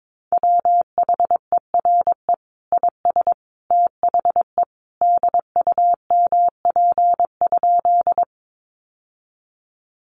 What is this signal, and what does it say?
Morse code, heard as a keyed tone, about 22 wpm, 700 Hz. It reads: WHERE IS THE DUMP?
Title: Sentences from Top 1300 Words 22wpm